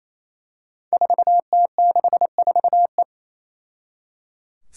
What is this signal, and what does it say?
Morse code, heard as a keyed tone, about 27 wpm, 700 Hz. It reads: 4T64E
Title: Five Challenging Letter-Number Combinations 28wpm